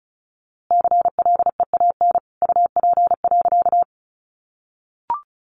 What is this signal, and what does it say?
Morse code, heard as a keyed tone, about 34 wpm, 700 Hz. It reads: CLEAN UP.
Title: A Sound Of Thunder 35wpm